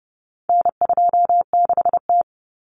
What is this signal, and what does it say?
Morse code, heard as a keyed tone, about 30 wpm, 700 Hz. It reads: N26T
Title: Common Cut Numbers - 4-Digit 30wpm